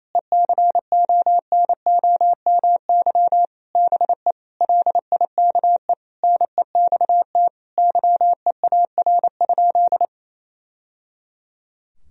Morse code, heard as a keyed tone, about 28 wpm, 700 Hz. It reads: ECONOMY BE LIKE NEXT YEAR?
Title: Sentences from Top 1000 Words - Review Part 3 - 28wpm